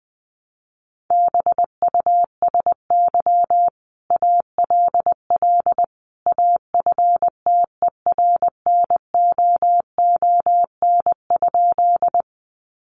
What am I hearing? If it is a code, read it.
Morse code, heard as a keyed tone, about 20 wpm, 700 Hz. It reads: BUSY ALL AFTERNOON?
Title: Sentences from Top 1500 Words 20wpm